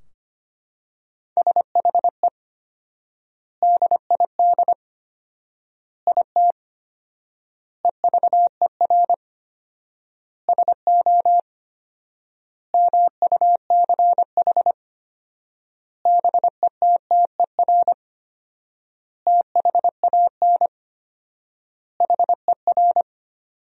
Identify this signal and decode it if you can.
Morse code, heard as a keyed tone, about 25 wpm, 700 Hz. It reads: SHE DID IT EVER SO MUCH BETTER THAN HER
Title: 4x Word-Spacing - Sentences from Top 400 Words - 25wpm